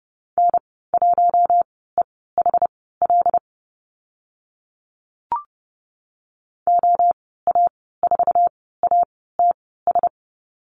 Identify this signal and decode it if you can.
Morse code, heard as a keyed tone, about 30 wpm, spaced as for 17 wpm, 700 Hz. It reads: N1EHL OA4ATS
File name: Lesson 14g - Number 2 - Call Signs 30wpm